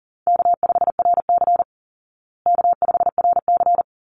Morse code, heard as a keyed tone, about 38 wpm, 700 Hz. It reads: K5RC K5RC
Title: Call Signs Repeated 3x - 40wpm